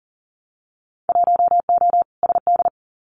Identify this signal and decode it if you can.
Morse code, heard as a keyed tone, about 42 wpm, 700 Hz. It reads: 1O SD
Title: Winter Field Day Exchanges Encore 40wpm